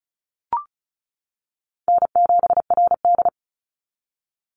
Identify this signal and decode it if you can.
Morse code, heard as a keyed tone, about 34 wpm, 700 Hz. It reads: N7RD
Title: Call Signs - 35wpm